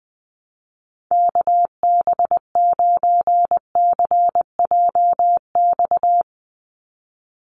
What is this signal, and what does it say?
Morse code, heard as a keyed tone, about 19 wpm, 700 Hz. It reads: KB9CJX
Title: USA Call Signs 20wpm